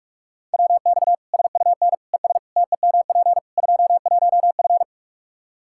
Morse code, heard as a keyed tone, about 47 wpm, 700 Hz. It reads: WX SUN ES TEMP 21F
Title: Practice QSO - Exchange 2 - Info - 45wpm